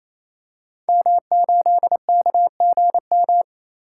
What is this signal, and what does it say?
Morse code, heard as a keyed tone, about 27 wpm, 700 Hz. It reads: M8KGM
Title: Code Groups - 5 Letter:Numbers - 28wpm